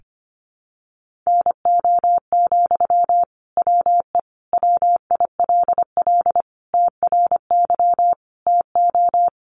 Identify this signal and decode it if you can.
Morse code, heard as a keyed tone, about 25 wpm, 700 Hz. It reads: NO, WE WILL TRY TO
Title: Sentences from Top 1100 Words 25wpm